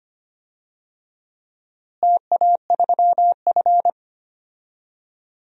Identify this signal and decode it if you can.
Morse code, heard as a keyed tone, about 25 wpm, 700 Hz. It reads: TA3F
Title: DXCC-100 Call Signs - 4 Characters - Repeated 2x - 25wpm